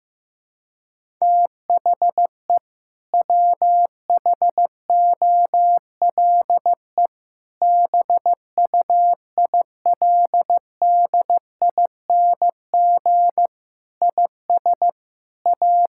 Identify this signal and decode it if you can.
Morse code, heard as a keyed tone, about 15 wpm, 700 Hz. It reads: THE WHOLE BUILDING IS A
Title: Sentences from Top 1000 Words - Review Part 5 - 15wpm